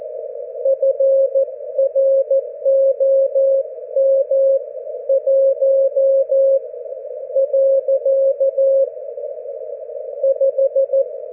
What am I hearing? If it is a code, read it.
Morse code, heard as a keyed tone, about 14 wpm, spaced as for 9 wpm, 540 Hz. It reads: FROM 1.5
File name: GW3UEP / GW4HXO 40mWRF 42mls 24Jly2012